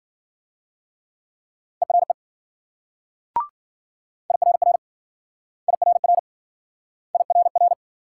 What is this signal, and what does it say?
Morse code, heard as a keyed tone, about 55 wpm, 700 Hz. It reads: ERE IRR IRR IRR